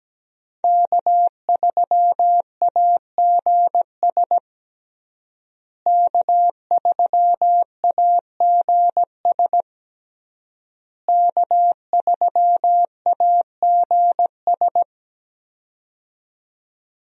Morse code, heard as a keyed tone, about 17 wpm, 700 Hz. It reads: K3AGS K3AGS K3AGS
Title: USA Call Signs - 5 Characters - Repeated 3x - 17wpm